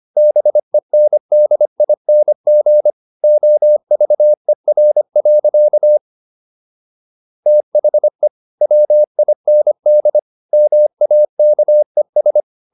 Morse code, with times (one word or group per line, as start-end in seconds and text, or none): 0.2-2.9 BENDING
3.2-6.0 OVER.
7.5-8.3 THE
8.6-10.2 WIND
10.5-12.4 MAKES